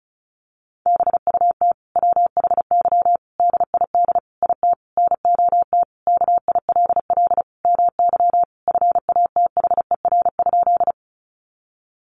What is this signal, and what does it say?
Morse code, heard as a keyed tone, about 34 wpm, 700 Hz. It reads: BUT WHY DID IT NOT KILL MY FATHER?
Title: Sentences from Top 1400 Words 35wpm